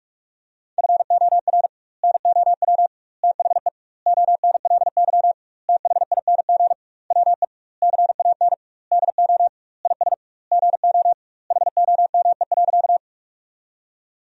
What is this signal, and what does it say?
Morse code, heard as a keyed tone, about 41 wpm, 700 Hz. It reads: FOR NOW THE ONLY THING WE CAN DO IS GO HOME.
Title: 2x Word-Spacing - Sentences from Top 200 Words - 45wpm